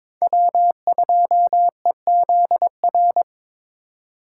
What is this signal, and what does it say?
Morse code, heard as a keyed tone, about 22 wpm, 700 Hz. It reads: W2EZR
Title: USA Call Signs - 5 Characters - 4x Character Spacing - 22wpm